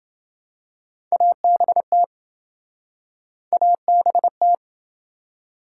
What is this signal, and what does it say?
Morse code, heard as a keyed tone, about 27 wpm, 700 Hz. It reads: ABT ABT